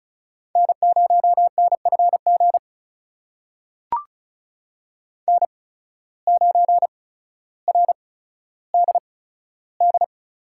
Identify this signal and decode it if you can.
Morse code, heard as a keyed tone, about 36 wpm, spaced as for 15 wpm, 700 Hz. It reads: N0NFG N9RDD
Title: USA Call Signs - 5 Characters - 8x Character Spacing - 35wpm